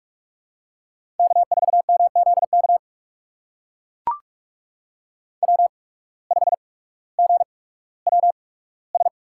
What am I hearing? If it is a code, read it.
Morse code, heard as a keyed tone, about 44 wpm, spaced as for 19 wpm, 690 Hz. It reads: K4MZK W5GWS
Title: USA Call Signs - 5 Characters - 8x Character Spacing - 45wpm